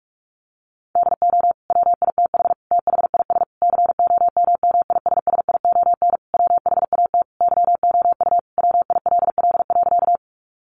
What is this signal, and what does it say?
Morse code, heard as a keyed tone, about 44 wpm, 700 Hz. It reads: DO WITH THIS COMMISSION WHAT YOU WILL.
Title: Sentences from Top 1400 Words 45wpm